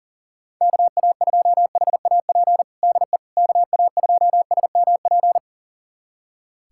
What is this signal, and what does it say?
Morse code, heard as a keyed tone, about 40 wpm, 700 Hz. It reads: KA2HAP DE KA2SMP
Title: USA Callsign DE Callsign - 6 Characters - 40wpm